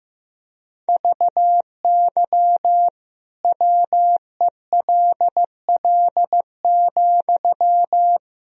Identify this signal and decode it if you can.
Morse code, heard as a keyed tone, about 15 wpm, 700 Hz. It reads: VY WELL,
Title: Sentences from Top 500 Words and Abbreviations 15wpm